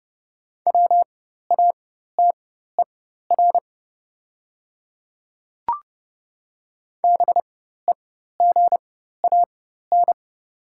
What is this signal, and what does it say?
Morse code, heard as a keyed tone, about 30 wpm, spaced as for 14 wpm, 690 Hz. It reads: WATER BEGAN
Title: 4x Character Spacing - Top 500 Words - 30wpm